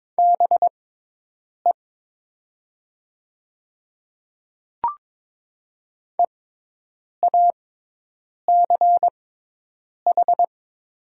Morse code, heard as a keyed tone, about 22 wpm, spaced as for 8 wpm, 700 Hz. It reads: BE EACH
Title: 6x Character Spacing - Top 100 Words - 22wpm